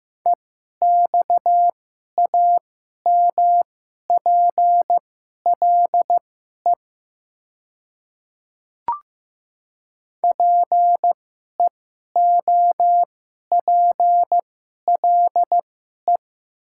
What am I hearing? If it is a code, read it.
Morse code, heard as a keyed tone, about 15 wpm, spaced as for 11 wpm, 700 Hz. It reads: EXAMPLE PEOPLE
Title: 2x Character Spacing - Top 300 Words - 15wpm